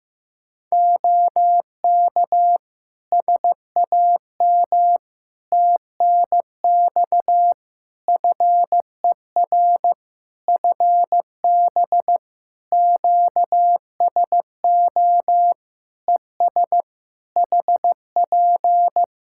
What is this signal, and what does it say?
Morse code, heard as a keyed tone, about 16 wpm, 700 Hz. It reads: OK SAM TNX FER FB QSO ES HP
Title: Practice QSO - Exchange 4 - Ending - 15wpm